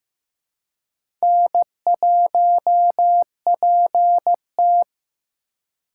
Morse code, heard as a keyed tone, about 15 wpm, 700 Hz. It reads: N1PT
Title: USA Call Signs - 4 Characters - 4x Character Spacing - 15wpm